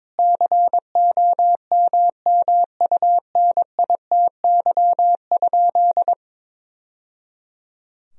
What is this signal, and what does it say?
Morse code, heard as a keyed tone, about 22 wpm, 700 Hz. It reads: COMMUNITY?
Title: Sentences from Top 1000 Words - Review Part 4 - 22wpm